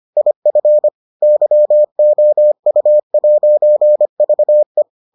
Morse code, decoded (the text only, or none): IF YOU'VE